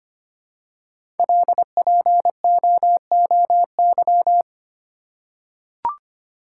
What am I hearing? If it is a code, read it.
Morse code, heard as a keyed tone, about 25 wpm, 700 Hz. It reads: LPOOY
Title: Code Groups - 5 Letters - 25wpm